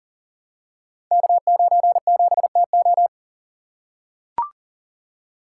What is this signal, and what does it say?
Morse code, heard as a keyed tone, about 41 wpm, 700 Hz. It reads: K97TO
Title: Code Groups - 5 Letter:Numbers - 40wpm